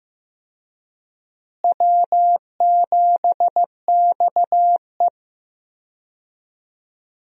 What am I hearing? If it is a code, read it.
Morse code, heard as a keyed tone, about 15 wpm, 700 Hz. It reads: W7XE
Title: USA Call Signs - 4 Characters - Repeated 3x - 15wpm